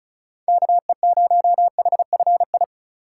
Morse code, heard as a keyed tone, about 43 wpm, 700 Hz. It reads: KE0HFI
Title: Speed Racing - USA Call Signs 35wpm